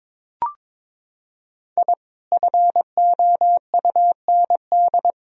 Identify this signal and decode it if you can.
Morse code, heard as a keyed tone, about 21 wpm, 700 Hz. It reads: I FOUND
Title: Sentences from Top 1000 Words - Review Part 2 - 22wpm